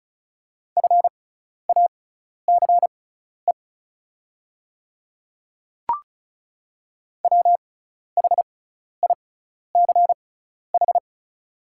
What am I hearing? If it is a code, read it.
Morse code, heard as a keyed tone, about 34 wpm, spaced as for 12 wpm, 700 Hz. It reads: FACE WHICH
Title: 6x Character Spacing - Top 300 Words - 35wpm